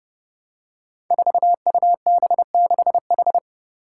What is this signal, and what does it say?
Morse code, heard as a keyed tone, about 32 wpm, 700 Hz. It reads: 4UB6H
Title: Five Challenging Letter-Number Combinations 30wpm